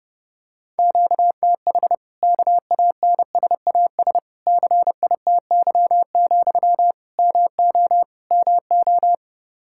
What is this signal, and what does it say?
Morse code, heard as a keyed tone, about 32 wpm, 700 Hz. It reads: QTH KANSAS CITY, MO MO
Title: Practice QSO - Exchange 1 - Beginning - 30wpm